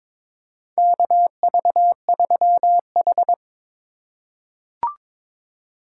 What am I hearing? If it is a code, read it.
Morse code, heard as a keyed tone, about 22 wpm, 700 Hz. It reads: KV3H
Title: USA Call Signs - 3x Character Spacing - 22wpm